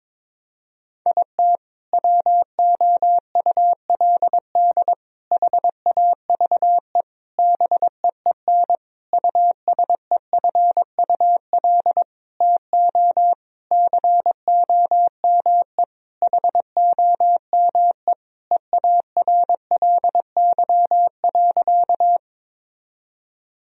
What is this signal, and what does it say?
Morse code, heard as a keyed tone, about 22 wpm, 700 Hz. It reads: IT WOULD HAVE BEEN USEFUL TO COME HOME EARLY.
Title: Sentences from Top 1000 Words - Review Part 4 - 22wpm